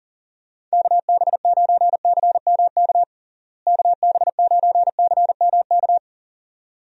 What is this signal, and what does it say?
Morse code, heard as a keyed tone, about 40 wpm, 700 Hz. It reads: KB9CMK KB9CMK
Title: USA Call Signs - 6 Characters - Repeated 2x - 40wpm